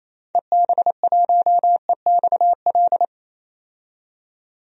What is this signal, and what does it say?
Morse code, heard as a keyed tone, about 27 wpm, 700 Hz. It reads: EB1EXL
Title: DXCC-100 Call Signs - 6 Characters - 28wpm